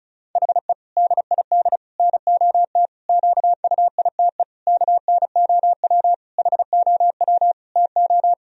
Morse code, heard as a keyed tone, about 35 wpm, 700 Hz. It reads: HE DID NOT QUITE KNOW HOW TO
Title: Sentences from Top 1000 Words - Review Part 1 - 35wpm